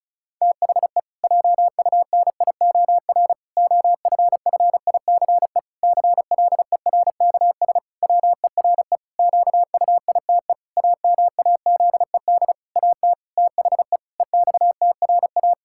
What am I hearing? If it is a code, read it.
Morse code, heard as a keyed tone, about 36 wpm, 700 Hz. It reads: THE JUNIOR OFFICE CLERKS WERE QUITE AMAZED AT THE EXTRA
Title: Pangram Challenge v2 - 35wpm